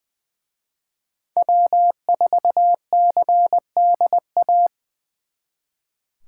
Morse code, heard as a keyed tone, about 22 wpm, 700 Hz. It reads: W4CDA
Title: Speed Racing - USA 1x3 Call Signs 20wpm